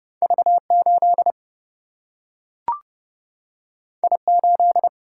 Morse code, heard as a keyed tone, about 32 wpm, 700 Hz. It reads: V 8 I 8
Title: Single-Letter Single-Number 30wpm